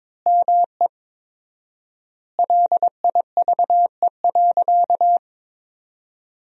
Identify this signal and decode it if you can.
Morse code, heard as a keyed tone, about 21 wpm, 700 Hz. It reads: ME LIVE.
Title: 4x Word-Spacing - Sentences from Top 300 Words - 22wpm